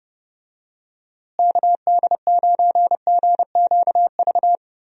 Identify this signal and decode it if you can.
Morse code, heard as a keyed tone, about 30 wpm, 700 Hz. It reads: KD9GQV
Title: USA Call Signs - 6 Characters - Repeated 3x - 30wpm